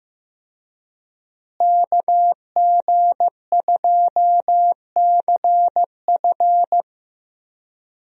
Morse code, heard as a keyed tone, about 14 wpm, 700 Hz. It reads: KG2CF
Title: USA Call Signs - 5 Characters - Repeated 3x - 15wpm